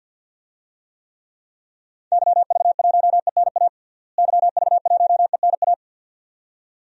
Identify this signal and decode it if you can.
Morse code, heard as a keyed tone, about 50 wpm, 700 Hz. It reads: YV1ENA YV1ENA
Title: DXCC-100 Call Signs - 6 Characters - Repeated 2x - 50wpm